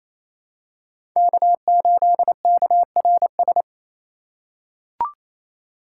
Morse code, heard as a keyed tone, about 28 wpm, 700 Hz. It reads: K8KRS
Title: USA Call Signs - 5 Characters - 4x Character Spacing - 28wpm